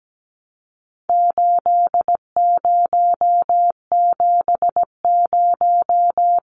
Morse code, heard as a keyed tone, about 17 wpm, 700 Hz. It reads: 8070
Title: Numbers - 4-Digit 17wpm